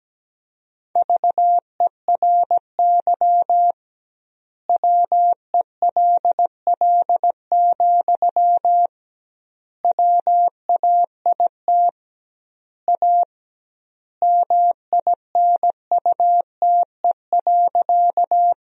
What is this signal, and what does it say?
Morse code, heard as a keyed tone, about 17 wpm, 700 Hz. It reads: VERY WELL, WAIT A MINUTE.
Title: 2x Word-Spacing - Sentences from Top 500 Words - 17wpm